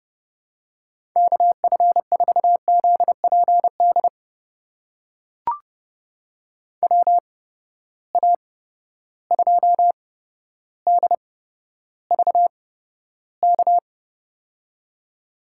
Morse code, heard as a keyed tone, about 31 wpm, spaced as for 13 wpm, 700 Hz. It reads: KF4ZPD WA2DVK
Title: USA Call Signs - 6 Characters - 8x Character Spacing - 30wpm